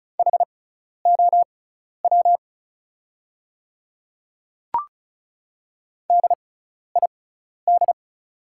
Morse code, heard as a keyed tone, about 36 wpm, spaced as for 12 wpm, 700 Hz. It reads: HOW DID
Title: 6x Character Spacing - Top 100 Words - 35wpm